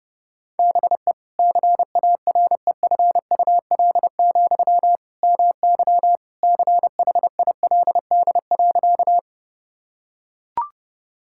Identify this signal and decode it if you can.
Morse code, heard as a keyed tone, about 32 wpm, 700 Hz. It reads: BE CAREFUL, MY CHILD.
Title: Sentences from Top 1600 Words 30wpm